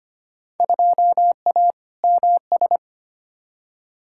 Morse code, heard as a keyed tone, about 25 wpm, 700 Hz. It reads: 2A MS
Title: Field Day Exchanges - 25wpm